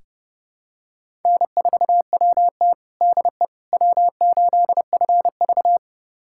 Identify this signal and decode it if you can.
Morse code, heard as a keyed tone, about 30 wpm, 700 Hz. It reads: N4WT DE W8FV
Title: USA Callsign DE Callsign - 4 Characters - 30wpm